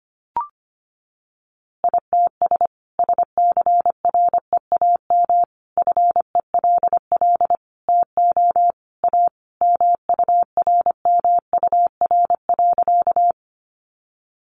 Morse code, heard as a keyed tone, about 25 wpm, 700 Hz. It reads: ITS SCREAM FELL TO A MURMUR.
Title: A Sound Of Thunder 25wpm